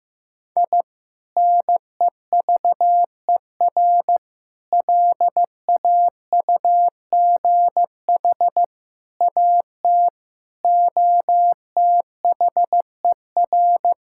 Morse code, heard as a keyed tone, about 15 wpm, 700 Hz. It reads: I NEVER LAUGH AT OTHER
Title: Sentences from Top 1000 Words - Review Part 6 - 15wpm